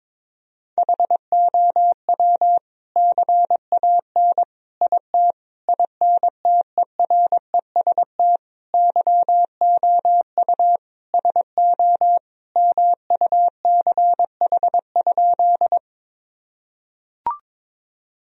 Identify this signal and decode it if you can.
Morse code, heard as a keyed tone, about 22 wpm, 700 Hz. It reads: HOW CAN IT INTEREST YOU SO MUCH?
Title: Sentences from Top 1000 Words - Review Part 2 - 22wpm